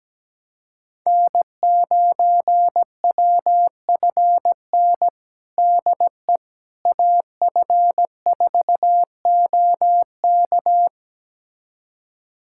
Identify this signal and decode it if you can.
Morse code, heard as a keyed tone, about 17 wpm, 700 Hz. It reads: N9WFN DE AF4OK
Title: USA Callsign DE Callsign - 5 Characters - 17wpm